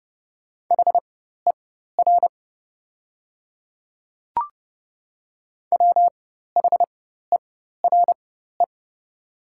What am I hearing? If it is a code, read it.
Morse code, heard as a keyed tone, about 30 wpm, spaced as for 14 wpm, 700 Hz. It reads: HER WHERE
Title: 4x Character Spacing - Top 200 Words - 30wpm